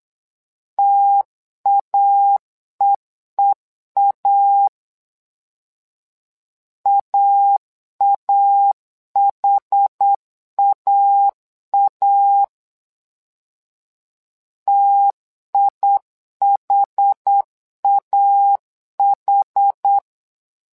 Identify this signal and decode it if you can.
Morse code, heard as a keyed tone, about 8 wpm, 800 Hz. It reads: TAEEA AAHAA TIHAH
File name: LEZIONI / 2 UHDN / 2 UH 8wpm 2129